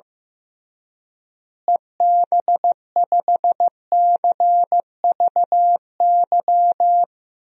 Morse code, heard as a keyed tone, about 16 wpm, 700 Hz. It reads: EB5CVY
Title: DXCC-100 Call Signs - 6 Characters - Repeated 2x - 15wpm